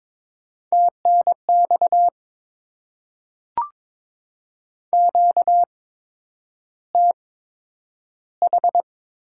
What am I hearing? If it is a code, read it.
Morse code, heard as a keyed tone, about 22 wpm, spaced as for 9 wpm, 700 Hz. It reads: TNX QTH
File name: Top 20 QSO Elements - 8x Character Spacing - 22wpm